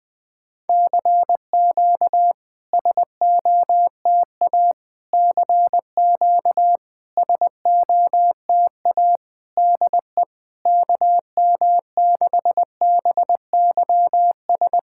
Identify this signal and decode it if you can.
Morse code, heard as a keyed tone, about 20 wpm, 690 Hz. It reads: CQ SOTA CQ SOTA DE KM6BYS
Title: SOTA QSO Practice 20wpm